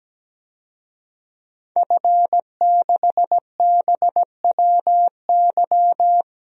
Morse code, heard as a keyed tone, about 17 wpm, 700 Hz. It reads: F6BWY
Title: DXCC-100 Call Signs - 5 Characters - 17wpm